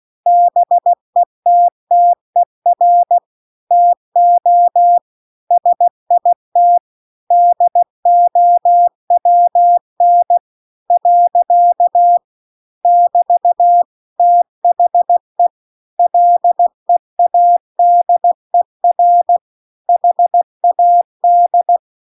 Morse code, heard as a keyed tone, about 16 wpm, 700 Hz. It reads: BETTER TO SIT DOWN . <BT> THE LEADER HAD